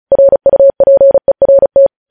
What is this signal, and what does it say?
Morse code, heard as a keyed tone, about 36 wpm, 550 Hz. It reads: RUPERT
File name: podcast / News headlines-1x-35x35